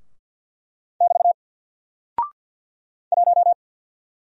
Morse code, heard as a keyed tone, about 48 wpm, 700 Hz. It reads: <BT> 1
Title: Single Character (Letter:Number:Punctuation:Prosign) 50wpm